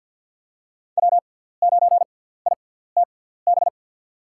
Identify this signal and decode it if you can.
Morse code, heard as a keyed tone, about 47 wpm, spaced as for 17 wpm, 700 Hz. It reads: W9ITB
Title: USA Call Signs - 6x Character Spacing - 50wpm